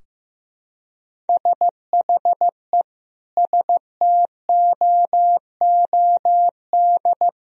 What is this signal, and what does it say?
Morse code, heard as a keyed tone, about 16 wpm, 700 Hz. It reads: SHE STOOD
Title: Sentences from Top 1000 Words - Review Part 2 - 15wpm